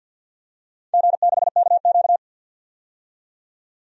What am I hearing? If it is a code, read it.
Morse code, heard as a keyed tone, about 50 wpm, 700 Hz. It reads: G6KQ
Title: DXCC-100 Call Signs - 4 Characters - 50wpm